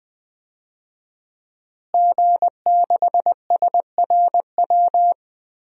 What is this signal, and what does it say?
Morse code, heard as a keyed tone, about 20 wpm, 700 Hz. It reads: G6SRW
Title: DXCC-100 Call Signs - 5 Characters - Repeated 2x - 20wpm